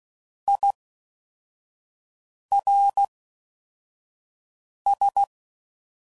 Morse code, heard as a keyed tone, about 16 wpm, 800 Hz. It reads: I R S
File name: Prefixes / FLs & THs / Fast TH Prefixes